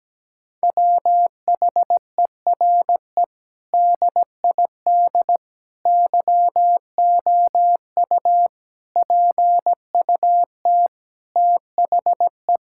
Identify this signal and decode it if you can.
Morse code, heard as a keyed tone, about 17 wpm, 700 Hz. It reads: WHERE DID YOU PUT THE
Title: Sentences from Top 1600 Words 17wpm